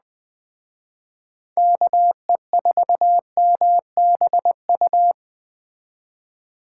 Morse code, heard as a keyed tone, about 21 wpm, 700 Hz. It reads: KE4MBU